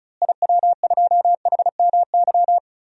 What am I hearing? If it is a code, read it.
Morse code, heard as a keyed tone, about 34 wpm, 700 Hz. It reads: IW2HMY
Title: DXCC-100 Call Signs - 6 Characters - Repeated 2x - 35wpm